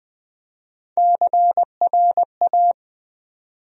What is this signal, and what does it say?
Morse code, heard as a keyed tone, about 20 wpm, 700 Hz. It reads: CRA
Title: USA Call Sign Postfixes - 3 Letters 20wpm